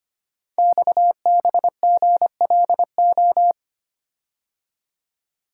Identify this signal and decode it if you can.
Morse code, heard as a keyed tone, about 25 wpm, 700 Hz. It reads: XBGLO
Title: Code Groups - 5 Letter:Numbers - 25wpm